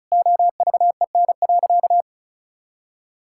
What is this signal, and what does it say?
Morse code, heard as a keyed tone, about 34 wpm, 700 Hz. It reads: OVEN.
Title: Sentences from Top 1000 Words - Review Part 6 - 35wpm